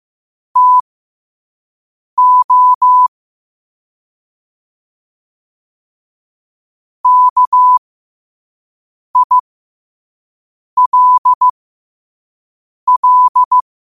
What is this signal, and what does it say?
Morse code, heard as a keyed tone, about 15 wpm, spaced as for 5 wpm, 1000 Hz. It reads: TO KILL
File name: qotdfiles / 5wpm